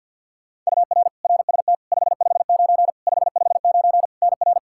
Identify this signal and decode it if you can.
Morse code, heard as a keyed tone, about 46 wpm, 700 Hz. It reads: UR RST 559 559 NR
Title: Practice QSO - Exchange 1 - Beginning - 50wpm